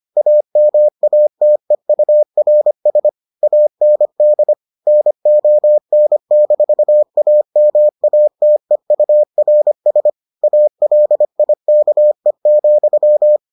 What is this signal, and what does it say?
Morse code, heard as a keyed tone, about 25 wpm, 600 Hz. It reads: AMATEURS AND NON-AMATEURS ALIKE,